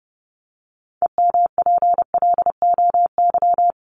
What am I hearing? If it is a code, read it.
Morse code, heard as a keyed tone, about 28 wpm, 700 Hz. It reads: EMPLOY
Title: Top 1751-2000 Words 30wpm